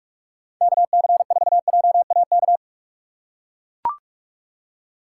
Morse code, heard as a keyed tone, about 41 wpm, 700 Hz. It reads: KC4JAK
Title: USA Call Signs - 6 Characters - 2x Character Spacing - 45wpm